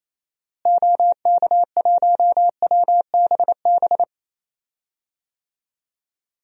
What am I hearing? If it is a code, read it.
Morse code, heard as a keyed tone, about 28 wpm, 700 Hz. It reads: OK1WBB
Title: DXCC-100 Call Signs - 6 Characters - Repeated 2x - 28wpm